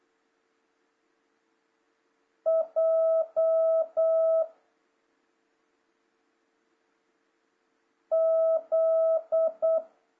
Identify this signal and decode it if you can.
Morse code, heard as a keyed tone, about 8 wpm, 650 Hz. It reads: J Z
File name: jh8cbh / 2026cw